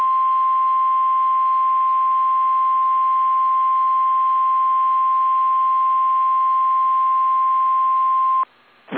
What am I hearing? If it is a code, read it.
Emergency Alert System (SAME) tones (the 1050 Hz attention tone). It is heard from 0.2 to 8.4 s.